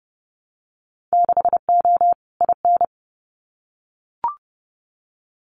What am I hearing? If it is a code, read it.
Morse code, heard as a keyed tone, about 29 wpm, 700 Hz. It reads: 6O IN
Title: Winter Field Day Exchanges Encore 30wpm